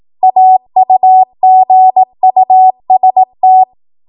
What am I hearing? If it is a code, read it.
Morse code, heard as a keyed tone, about 18 wpm, 750 Hz. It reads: AUGUST
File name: Morse / 18WPM